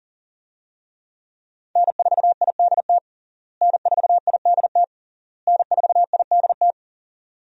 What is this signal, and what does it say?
Morse code, heard as a keyed tone, about 41 wpm, 700 Hz. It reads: N4IDT N4IDT N4IDT
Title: USA Call Signs - 5 Characters - Repeated 3x - 40wpm